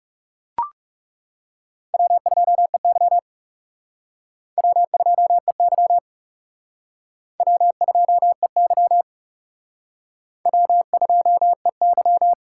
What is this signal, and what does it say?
Morse code, heard as a keyed tone, about 37 wpm, 700 Hz. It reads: W2EY W2EY W2EY W2EY
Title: Speed Racing - USA Call Signs 30wpm